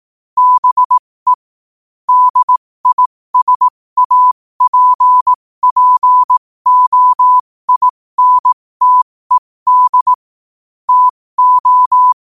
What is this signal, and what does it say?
Morse code, heard as a keyed tone, about 18 wpm, spaced as for 15 wpm, 1000 Hz. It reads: BE DISAPPOINTED TO